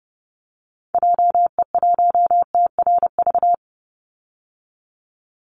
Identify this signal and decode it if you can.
Morse code, heard as a keyed tone, about 30 wpm, 700 Hz. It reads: JE1TRV
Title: Call Signs Repeated 3x - 30wpm